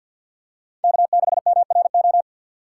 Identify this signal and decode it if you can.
Morse code, heard as a keyed tone, about 47 wpm, 700 Hz. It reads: K6MRO
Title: USA Call Signs - 5 Characters - 4x Character Spacing - 50wpm